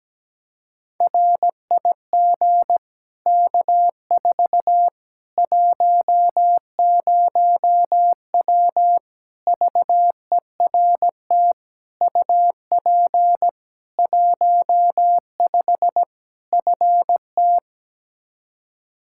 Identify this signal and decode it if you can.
Morse code, heard as a keyed tone, about 17 wpm, 700 Hz. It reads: RIG K4 10W VERT UP 15 FT
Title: Practice QSO - Exchange 2 - Info - 17wpm